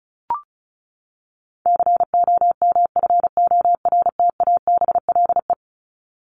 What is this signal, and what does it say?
Morse code, heard as a keyed tone, about 34 wpm, 700 Hz. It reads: COMFORTABLE
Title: Top 751-1000 Words 35wpm